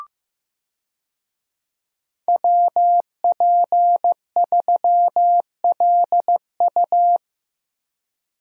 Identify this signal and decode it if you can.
Morse code, heard as a keyed tone, about 15 wpm, 700 Hz. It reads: WP3LU
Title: USA Call Signs - 5 Characters - Repeated 2x - 15wpm